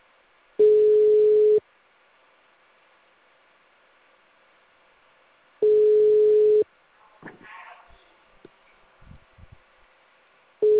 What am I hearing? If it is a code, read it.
Ringback tone (Europe, 425 Hz).